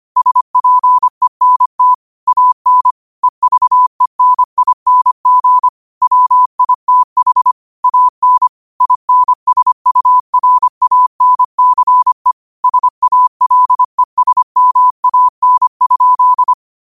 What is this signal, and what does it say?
Morse code, heard as a keyed tone, about 25 wpm, 1000 Hz. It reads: SPENT AN EVENING WITH AN INSURANCE SALESMAN?
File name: qotdfiles / 25wpm